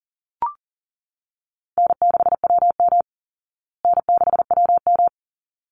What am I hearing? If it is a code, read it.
Morse code, heard as a keyed tone, about 40 wpm, 700 Hz. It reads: N6WM N6WM
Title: Easy Call Signs Repeated 2x - 40wpm